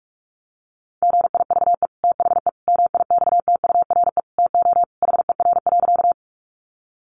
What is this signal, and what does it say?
Morse code, heard as a keyed tone, about 44 wpm, 700 Hz. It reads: GIVE THE MIXTURE TO HER.
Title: Sentences from Top 1500 Words 45wpm